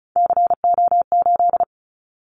Morse code, heard as a keyed tone, about 34 wpm, 700 Hz. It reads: CO8
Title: Callsign Prefixes for Top 100 DXCC - 35wpm